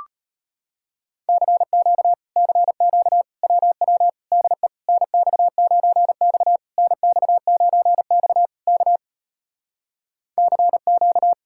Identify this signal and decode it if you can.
Morse code, heard as a keyed tone, about 33 wpm, 700 Hz. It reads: CQ CQ WW DE NX9X NX9X K CQ